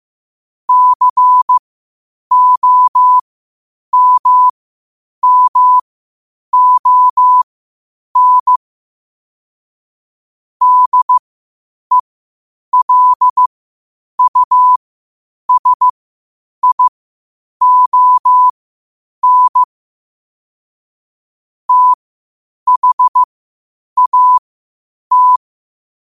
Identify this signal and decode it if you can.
Morse code, heard as a keyed tone, about 15 wpm, spaced as for 8 wpm, 1000 Hz. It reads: COMMON DELUSION THAT
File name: qotdfiles / 8wpm